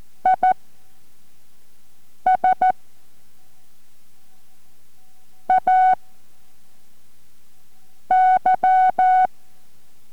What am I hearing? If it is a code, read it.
Morse code, heard as a keyed tone, about 14 wpm, spaced as for 3 wpm, 750 Hz. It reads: I SAY